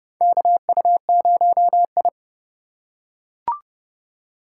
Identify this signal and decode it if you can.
Morse code, heard as a keyed tone, about 30 wpm, 700 Hz. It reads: KU0I
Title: USA Call Signs - 4 Characters - 6x Character Spacing - 30wpm